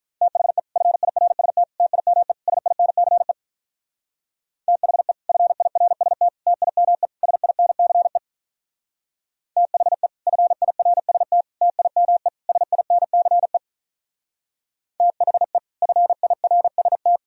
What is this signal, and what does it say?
Morse code, heard as a keyed tone, about 44 wpm, 700 Hz. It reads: THE FIRST TIME SINCE THE FIRST TIME SINCE THE FIRST TIME SINCE THE FIRST
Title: Speed-Racing - Sets of 4 Words 35wpm